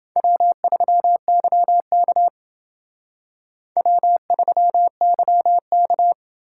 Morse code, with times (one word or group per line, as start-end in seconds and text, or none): 0.2-2.3 W3YK
3.8-6.1 W3YK